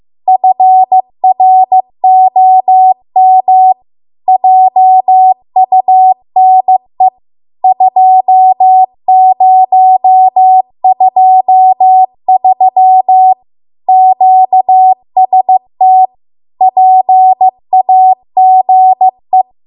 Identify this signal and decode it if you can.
Morse code, heard as a keyed tone, about 15 wpm, 750 Hz. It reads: FROM JUNE 2023 QST PAGE